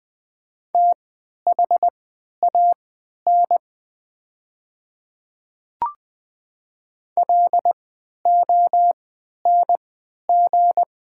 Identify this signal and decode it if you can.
Morse code, heard as a keyed tone, about 20 wpm, spaced as for 11 wpm, 700 Hz. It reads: THAN LONG
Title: 3x Character Spacing - Top 100 Words - 20wpm